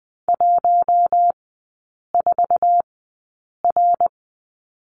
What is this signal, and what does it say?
Morse code, heard as a keyed tone, about 20 wpm, spaced as for 8 wpm, 700 Hz. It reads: 14R
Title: 053 Lesson 10e - Letter r - Three Characters 20wpm